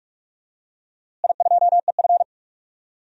Morse code, heard as a keyed tone, about 46 wpm, 700 Hz. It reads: I2EF
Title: DXCC-100 Call Signs - 4 Characters - Repeated 3x - 45wpm